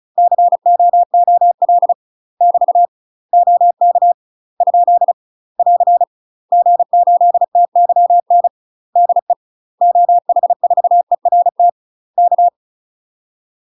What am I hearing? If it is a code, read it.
Morse code, heard as a keyed tone, about 34 wpm, 700 Hz. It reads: COOL <BT> OK ? <AR> G8TYN DE OH4ERT K
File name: UK MASTER QSOs TRAINING / 35WPM / UK MASTER QSOs 0043